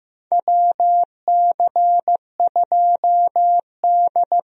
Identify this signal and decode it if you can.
Morse code, heard as a keyed tone, about 15 wpm, 700 Hz. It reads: WC2D
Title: USA Call Signs - 4 Characters - Repeated 3x - 15wpm